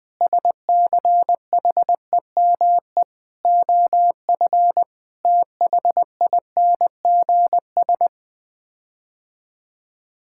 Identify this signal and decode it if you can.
Morse code, heard as a keyed tone, about 21 wpm, 700 Hz. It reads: SCHEME OF THINGS
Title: Sets of 6 Words - 20wpm